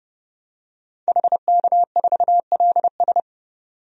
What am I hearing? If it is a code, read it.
Morse code, heard as a keyed tone, about 29 wpm, 700 Hz. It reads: HK4LS
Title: DXCC-100 Call Signs - 5 Characters - 30wpm